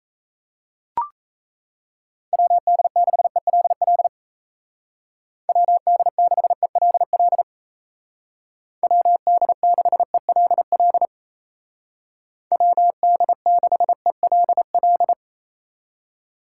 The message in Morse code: WD6ELL WD6ELL WD6ELL WD6ELL

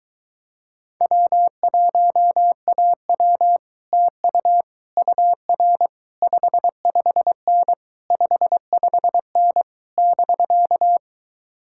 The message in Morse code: W1AW TU UR 55N 55N <BK>

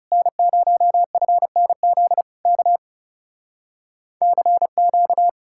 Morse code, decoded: N0FNZ K CQ

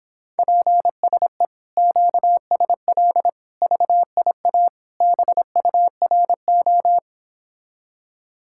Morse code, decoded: PSE QSL VIA BURO

